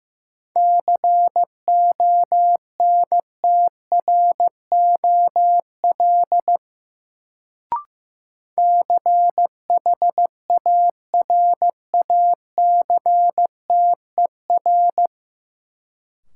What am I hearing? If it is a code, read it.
Morse code, heard as a keyed tone, about 15 wpm, 700 Hz. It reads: CONTROL CHARACTER
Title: Sentences from Top 1200 Words - New Word Drill Alt - 15wpm